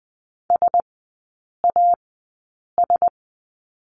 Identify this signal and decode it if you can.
Morse code, heard as a keyed tone, about 20 wpm, spaced as for 8 wpm, 700 Hz. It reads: SAS
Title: 060 Lesson 11e - Letter h - Three Characters 20wpm